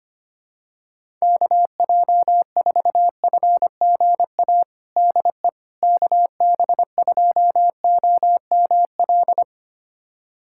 Morse code, heard as a keyed tone, about 25 wpm, 700 Hz. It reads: KJ4FGA DE KB2OML
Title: USA Callsign DE Callsign - 6 Characters - 25wpm